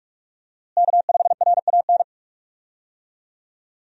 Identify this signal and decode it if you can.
Morse code, heard as a keyed tone, about 47 wpm, 700 Hz. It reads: K5RAN
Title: USA Call Signs - 5 Characters - Repeated 3x - 45wpm